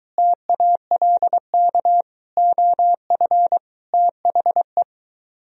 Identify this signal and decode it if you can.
Morse code, heard as a keyed tone, about 23 wpm, 700 Hz. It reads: TALK OF THE